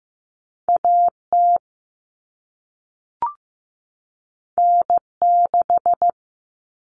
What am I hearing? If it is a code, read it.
Morse code, heard as a keyed tone, about 15 wpm, 700 Hz. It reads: AT N6
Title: Common Cut Numbers - 2-Digit 15wpm